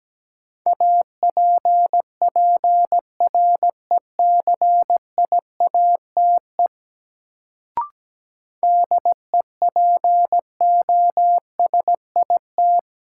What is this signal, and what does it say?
Morse code, heard as a keyed tone, about 17 wpm, 700 Hz. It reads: APPRECIATE DEPOSIT
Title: Sentences from Top 1300 Words - New Word Drill - 17wpm